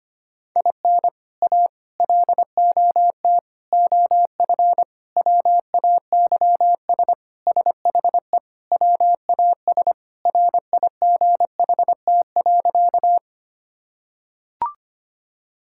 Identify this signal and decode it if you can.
Morse code, heard as a keyed tone, about 25 wpm, 700 Hz. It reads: IN A LOT OF WAYS SHE WAS RIGHT.
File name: practice / 25wpm / Sentences with 4-Word Phrases - 25wpm